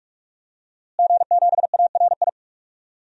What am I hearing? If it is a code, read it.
Morse code, heard as a keyed tone, about 44 wpm, 700 Hz. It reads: G7ARI
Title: DXCC-100 Call Signs - 5 Characters - Repeated 3x - 45wpm